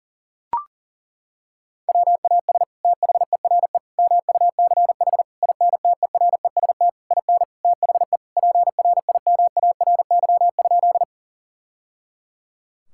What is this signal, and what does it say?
Morse code, heard as a keyed tone, about 37 wpm, 700 Hz. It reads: WAS THERE MUCH INTEREST IN THE PRIMARY?
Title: Sentences from Top 1000 Words - Review Part 5 - 40wpm